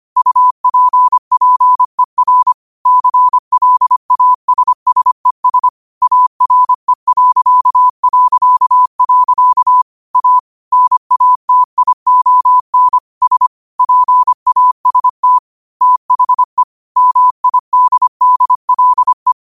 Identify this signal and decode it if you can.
Morse code, heard as a keyed tone, about 25 wpm, 1000 Hz. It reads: UPPER CLASSES ARE... A NATION S PAST THE MIDDLE
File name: qotdfiles / 25wpm